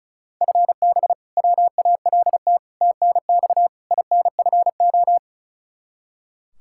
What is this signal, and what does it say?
Morse code, heard as a keyed tone, about 34 wpm, 700 Hz. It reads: FB WALT TNX INFO
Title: Practice QSO - Exchange 3 - More Info - 35wpm